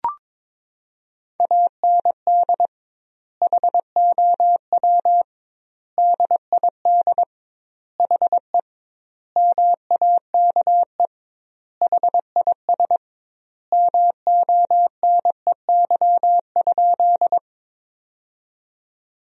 Morse code, heard as a keyed tone, about 22 wpm, 700 Hz. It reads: AND HOW DID HE MAKE HIS MONEY?
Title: 2x Word-Spacing - Sentences from Top 400 Words - 22wpm